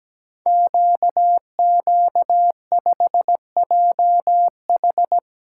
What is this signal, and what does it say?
Morse code, heard as a keyed tone, about 17 wpm, 690 Hz. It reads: QQ5JH
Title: Code Groups - 5 Letter:Numbers - 17wpm